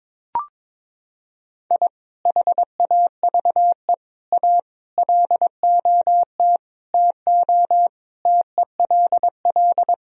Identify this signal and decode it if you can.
Morse code, heard as a keyed tone, about 22 wpm, 700 Hz. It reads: I HAVE A LOT TO TELL
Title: Sentences from Top 1000 Words - Review Part 3 - 22wpm